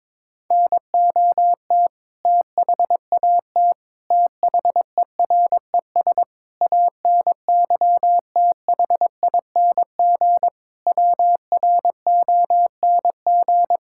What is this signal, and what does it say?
Morse code, heard as a keyed tone, about 22 wpm, 700 Hz. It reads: NOT THAT THERES ANYTHING WRONG
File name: practice / 22wpm / Sets of 7 Words - 22wpm